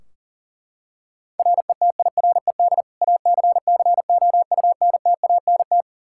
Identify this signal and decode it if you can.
Morse code, heard as a keyed tone, about 40 wpm, 700 Hz. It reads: RETIRED ACCOUNTANT
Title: Practice QSO - Exchange 3 - More Info - 40wpm